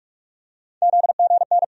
Morse code, heard as a keyed tone, about 42 wpm, 700 Hz. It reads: ZGN
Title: USA Call Sign Postfixes - 3 Letters 45wpm